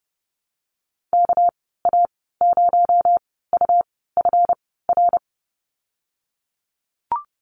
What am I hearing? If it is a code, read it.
Morse code, heard as a keyed tone, about 30 wpm, spaced as for 17 wpm, 700 Hz. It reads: KA0UFR
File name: Lesson 37g - Letter x - Call Signs 30wpm